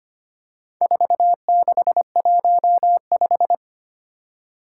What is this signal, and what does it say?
Morse code, heard as a keyed tone, about 25 wpm, 700 Hz. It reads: 4615
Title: Four Challenging Letter-Number Combinations 25wpm